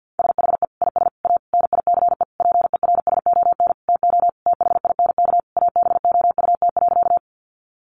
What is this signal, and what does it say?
Morse code, heard as a keyed tone, about 50 wpm, 700 Hz. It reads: SHE IS A NICE PERSON TO THINK ABOUT.
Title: Sentences from Top 1100 Words 50wpm